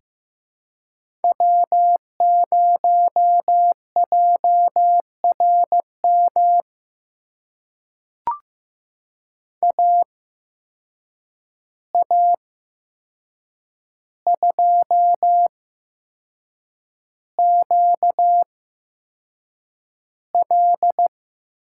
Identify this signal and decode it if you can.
Morse code, heard as a keyed tone, about 15 wpm, spaced as for 6 wpm, 700 Hz. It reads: W0JRM AA2QL
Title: USA Call Signs - 5 Characters - 8x Character Spacing - 15wpm